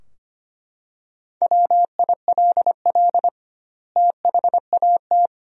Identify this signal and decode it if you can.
Morse code, heard as a keyed tone, about 25 wpm, 700 Hz. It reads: WILL THAT
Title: 2x Word-Spacing - Sentences from Top 100 Words - 25wpm